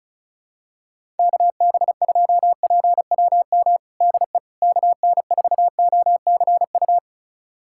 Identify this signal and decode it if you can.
Morse code, heard as a keyed tone, about 35 wpm, 700 Hz. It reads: KB2PWM DE KN4OCU
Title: USA Callsign DE Callsign - 6 Characters - 35wpm